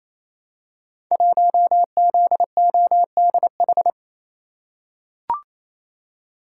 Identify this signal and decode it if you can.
Morse code, heard as a keyed tone, about 28 wpm, 700 Hz. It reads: 1ZODH
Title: Code Groups - 5 Letter:Numbers - 28wpm